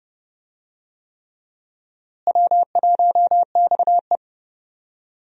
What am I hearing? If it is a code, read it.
Morse code, heard as a keyed tone, about 31 wpm, 700 Hz. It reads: W1XE
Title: USA Call Signs - 4 Characters - Repeated 3x - 30wpm